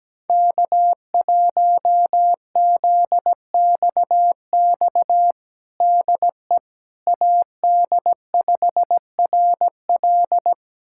Morse code, heard as a keyed tone, about 17 wpm, 700 Hz. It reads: K1ZXX DE AD5RL
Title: USA Callsign DE Callsign - 5 Characters - 17wpm